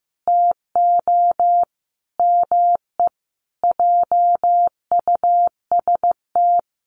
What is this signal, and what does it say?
Morse code, heard as a keyed tone, about 15 wpm, 700 Hz. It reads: TO ME JUST
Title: Sentences from Top 1200 Words 15wpm